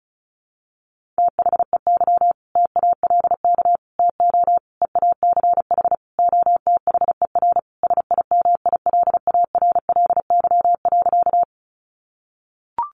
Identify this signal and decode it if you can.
Morse code, heard as a keyed tone, about 34 wpm, 700 Hz. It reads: THEY TALK TO EACH OTHER SIMILARLY.
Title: Sentences from Top 1400 Words 35wpm